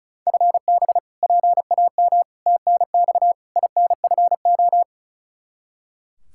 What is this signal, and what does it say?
Morse code, heard as a keyed tone, about 36 wpm, 700 Hz. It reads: FB PAM TNX INFO
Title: Practice QSO - Exchange 3 - More Info - 35wpm